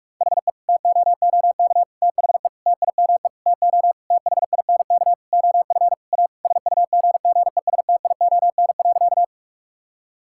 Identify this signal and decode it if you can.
Morse code, heard as a keyed tone, about 46 wpm, 700 Hz. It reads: HE TOOK THE TIME TO THINK OF A SUGGESTION.
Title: Sentences from Top 1600 Words 45wpm